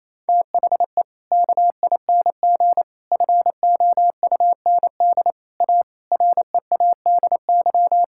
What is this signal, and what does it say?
Morse code, heard as a keyed tone, about 28 wpm, 700 Hz. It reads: THE KING FOUND A READY
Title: Sentences from Top 1000 Words - Review Part 5 - 28wpm